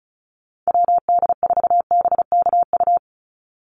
Morse code, heard as a keyed tone, about 34 wpm, 700 Hz. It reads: WD4BKU
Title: USA Call Signs 35wpm